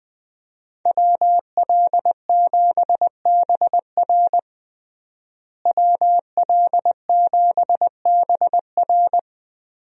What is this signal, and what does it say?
Morse code, heard as a keyed tone, about 21 wpm, 700 Hz. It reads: WL7BR WL7BR
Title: USA Call Signs - 5 Characters - Repeated 3x - 20wpm